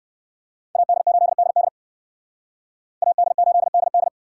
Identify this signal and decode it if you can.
Morse code, heard as a keyed tone, about 57 wpm, 700 Hz. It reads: AD8DD AD8DD